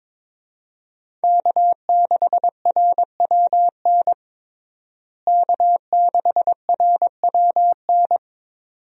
Morse code, heard as a keyed tone, about 22 wpm, 700 Hz. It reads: K6RWN K6RWN
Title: USA Call Signs - 5 Characters - Repeated 2x - 22wpm